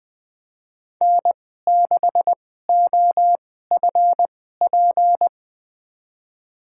Morse code, heard as a keyed tone, about 20 wpm, spaced as for 15 wpm, 700 Hz. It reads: N6OFP
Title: USA Call Signs - 5 Characters - 2x Character Spacing - 20wpm